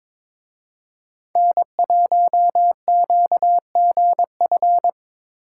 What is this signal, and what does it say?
Morse code, heard as a keyed tone, about 25 wpm, 700 Hz. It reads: N1QGF
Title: Speed Racing - USA 1x3 Call Signs 22wpm